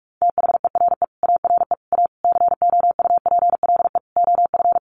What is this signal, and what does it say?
Morse code, heard as a keyed tone, about 44 wpm, 700 Hz. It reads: THERE ARE A COUPLE OF
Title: Sets of 5 Words Encore 45wpm